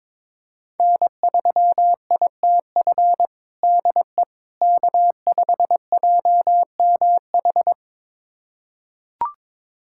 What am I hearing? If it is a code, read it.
Morse code, heard as a keyed tone, about 22 wpm, 700 Hz. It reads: N3ITF DE K5JMH
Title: USA Callsign DE Callsign - 5 Characters - 22wpm